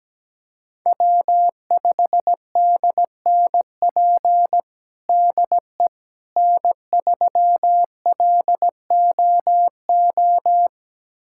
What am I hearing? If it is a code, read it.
Morse code, heard as a keyed tone, about 17 wpm, 700 Hz. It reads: W5DNP DE N3LOO
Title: USA Callsign DE Callsign - 5 Characters - 17wpm